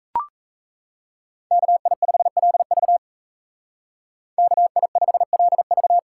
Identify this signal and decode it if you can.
Morse code, heard as a keyed tone, about 36 wpm, 700 Hz. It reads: KI5LV KI5LV